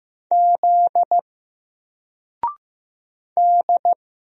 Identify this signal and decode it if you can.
Morse code, heard as a keyed tone, about 15 wpm, 700 Hz. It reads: Z D
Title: Single Letter-Number Alt 15wpm